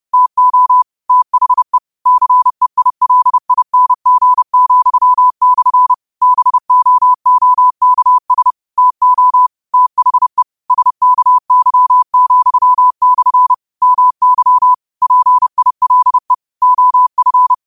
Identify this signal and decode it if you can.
Morse code, heard as a keyed tone, about 30 wpm, 1000 Hz. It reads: TO THE CEILING,/ BOOKS TO THE SKY,/ MY PILE OF